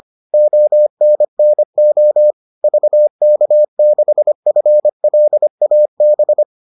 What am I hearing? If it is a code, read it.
Morse code, heard as a keyed tone, about 25 wpm, 600 Hz. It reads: ONNO VK6FLAB